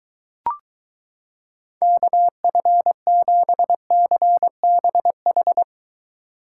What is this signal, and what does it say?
Morse code, heard as a keyed tone, about 23 wpm, 700 Hz. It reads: KF7CBH